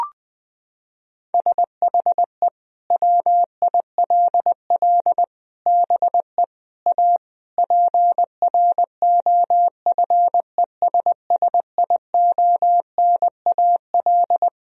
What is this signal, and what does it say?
Morse code, heard as a keyed tone, about 19 wpm, 700 Hz. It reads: SHE WILL BE A PROFESSIONAL
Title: Sentences from Top 1000 Words - Review Part 4 - 20wpm